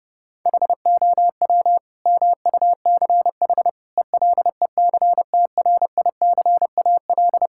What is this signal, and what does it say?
Morse code, heard as a keyed tone, about 30 wpm, 700 Hz. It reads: HOW MUCH ELECTRICAL
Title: Sentences from Top 1000 Words - Review Part 6 - 30wpm